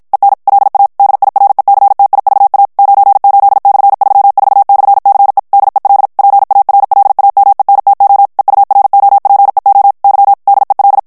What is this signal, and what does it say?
Morse code, heard as a keyed tone, about 53 wpm, 800 Hz. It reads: EN LA DIRECTIVA 98/34/CE DEL PARLAMENTO EUROPEO Y DEL
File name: morse mp3 / ESPAGNOL TEXTE EN CLAIR 2 VITESSE 50 MOTS MINUTES